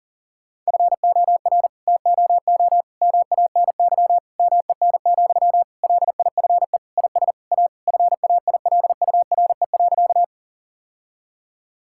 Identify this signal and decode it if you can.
Morse code, heard as a keyed tone, about 37 wpm, 700 Hz. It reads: FOR TOO MANY MEN, LIFE IS A FAILURE.
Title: Sentences from Top 1000 Words - Review Part 6 - 40wpm